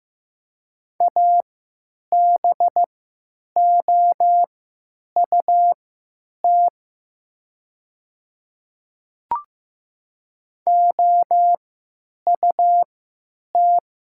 Morse code, heard as a keyed tone, about 15 wpm, spaced as for 9 wpm, 700 Hz. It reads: ABOUT OUT
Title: 3x Character Spacing - Top 100 Words - 15wpm